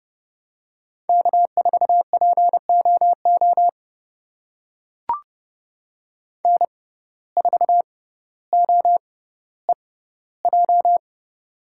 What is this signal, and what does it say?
Morse code, heard as a keyed tone, about 30 wpm, spaced as for 15 wpm, 700 Hz. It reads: K4POO N4OEJ
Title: USA Call Signs - 5 Characters - 6x Character Spacing - 30wpm